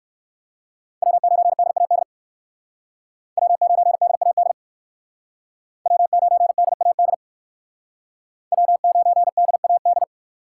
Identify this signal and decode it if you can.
Morse code, heard as a keyed tone, about 58 wpm, 700 Hz. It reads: W9DAD W9DAD W9DAD W9DAD